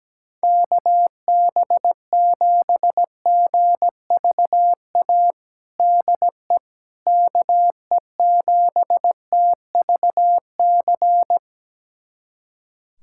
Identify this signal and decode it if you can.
Morse code, heard as a keyed tone, about 17 wpm, 700 Hz. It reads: KB7GVA DE KE7TVC
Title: USA Callsign DE Callsign - 6 Characters - 17wpm